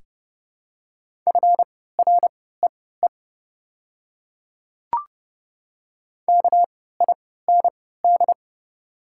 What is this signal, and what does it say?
Morse code, heard as a keyed tone, about 29 wpm, spaced as for 17 wpm, 690 Hz. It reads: FREE KIND
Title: 3x Character Spacing - Top 500 Words - 30wpm